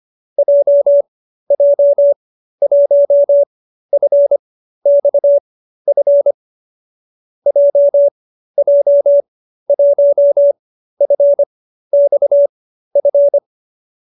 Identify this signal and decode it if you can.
Morse code, heard as a keyed tone, about 25 wpm, spaced as for 13 wpm, 570 Hz. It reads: JJ1FXF JJ1FXF
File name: qso208b 13